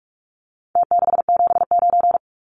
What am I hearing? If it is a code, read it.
Morse code, heard as a keyed tone, about 44 wpm, 700 Hz. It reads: T679
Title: Numbers - Contest Style - 4-Digit 45wpm